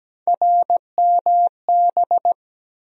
Morse code, heard as a keyed tone, about 17 wpm, 700 Hz. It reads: RMB
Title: USA Call Sign Postfixes - 3 Letters 17wpm